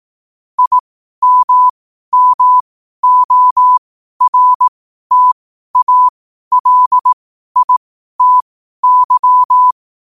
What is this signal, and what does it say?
Morse code, heard as a keyed tone, about 18 wpm, spaced as for 12 wpm, 1000 Hz. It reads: IMMORTALITY
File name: qotdfiles / 12wpm